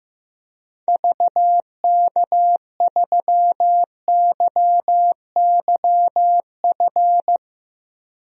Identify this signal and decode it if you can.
Morse code, heard as a keyed tone, about 16 wpm, 700 Hz. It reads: VK3YYF
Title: DXCC-100 Call Signs - 6 Characters - 15wpm